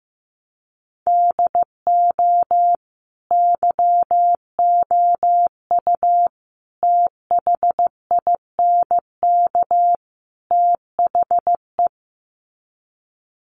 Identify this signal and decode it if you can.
Morse code, heard as a keyed tone, about 14 wpm, 700 Hz. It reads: DO YOU THINK THE
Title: Sets of 4 Words 15wpm